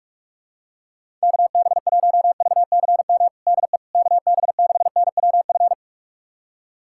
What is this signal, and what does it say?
Morse code, heard as a keyed tone, about 42 wpm, 700 Hz. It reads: KB1VCM DE KB6NWF
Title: USA Callsign DE Callsign - 6 Characters - 45wpm